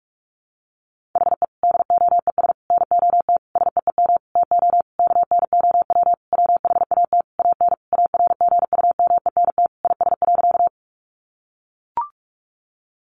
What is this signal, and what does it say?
Morse code, heard as a keyed tone, about 44 wpm, 700 Hz. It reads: HE DOES NOT SEEM TO KNOW WHAT AN ARGUMENT IS.
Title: Sentences from Top 1400 Words 45wpm